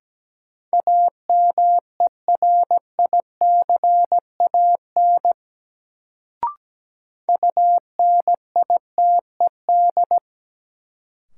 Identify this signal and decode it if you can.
Morse code, heard as a keyed tone, about 17 wpm, 700 Hz. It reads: AMERICAN UNITED
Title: Sets of 2 Words - New Word Drill - 17wpm